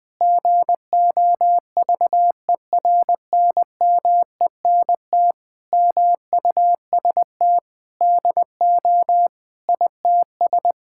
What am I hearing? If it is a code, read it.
Morse code, heard as a keyed tone, about 20 wpm, 700 Hz. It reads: GOVERNMENT MUST DO ITS
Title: Sentences from Top 1000 Words - Review Part 5 - 20wpm